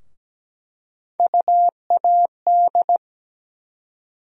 Morse code, heard as a keyed tone, about 17 wpm, 700 Hz. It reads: UAD
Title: USA Call Sign Postfixes - 3 Letters 17wpm